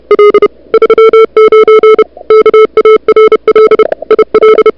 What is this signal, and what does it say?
Morse code, heard as a keyed tone, about 34 wpm, 440 Hz, with noise. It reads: L 39 KARL IL
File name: n6tr / Fab2007CWSprintHour1